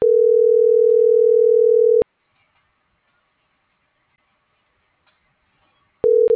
Ringback tone (North America), 440 + 480 Hz.